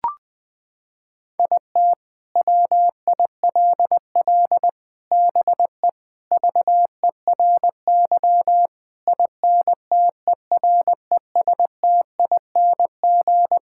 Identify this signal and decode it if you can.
Morse code, heard as a keyed tone, about 20 wpm, 700 Hz. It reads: IT WILL BE VERY INTERESTING
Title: Sentences with 7-Word Phrases - 20wpm